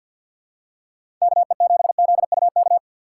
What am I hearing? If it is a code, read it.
Morse code, heard as a keyed tone, about 51 wpm, 700 Hz. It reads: KE7ZUK
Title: USA Call Signs - 6 Characters - 8x Character Spacing - 50wpm